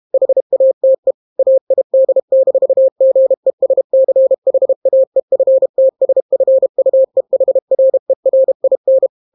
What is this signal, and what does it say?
Morse code, heard as a keyed tone, about 32 wpm, 530 Hz. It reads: HATE AID-GESCHAEFTSFUEHRERIN